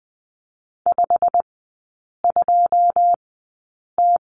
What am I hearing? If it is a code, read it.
Morse code, heard as a keyed tone, about 20 wpm, spaced as for 8 wpm, 700 Hz. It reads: 52T
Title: 100 Lesson 17e - Letter u - Three Characters 20wpm